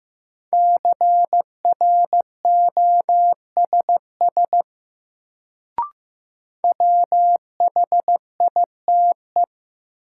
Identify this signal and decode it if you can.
Morse code, heard as a keyed tone, about 16 wpm, 700 Hz. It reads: CROSS WHITE
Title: Sentences from Top 300 Words - New Word Drill - 15wpm